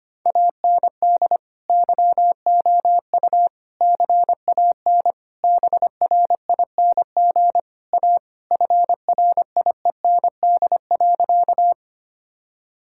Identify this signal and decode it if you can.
Morse code, heard as a keyed tone, about 25 wpm, 700 Hz. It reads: AND YOU CAN BRING A FRIEND.
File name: practice / 25wpm / Sentences with 2-Word Phrases - 25wpm